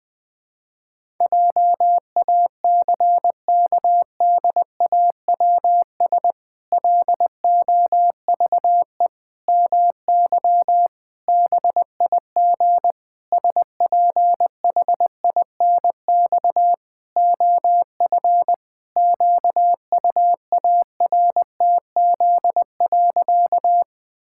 Morse code, heard as a keyed tone, about 20 wpm, 700 Hz. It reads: JACKDAWS LOVE MY BIG SPHINX OF QUARTZ.
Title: Pangram Challenge v2 - 20wpm